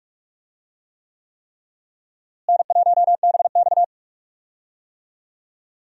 Morse code, heard as a keyed tone, about 46 wpm, 700 Hz. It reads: N1BX
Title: USA Call Signs - 4 Characters - Repeated 3x - 45wpm